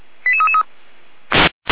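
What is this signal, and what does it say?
Five-tone selective call (ZVEI), heard as quick runs of five tones, tones of about 70 ms. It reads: 88333
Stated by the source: ZVEI five-tone selective call received via SDR.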